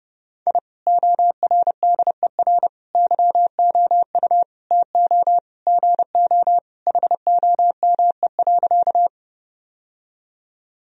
Morse code, heard as a keyed tone, about 32 wpm, 700 Hz. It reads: I ORDER YOU TO GO HOME.
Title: Sentences from Top 1000 Words - Review Part 2 - 30wpm